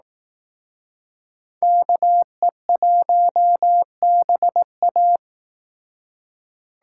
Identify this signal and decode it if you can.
Morse code, heard as a keyed tone, about 18 wpm, 700 Hz. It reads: KE1BA
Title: Speed Racing - USA 2x2 Call Signs 15wpm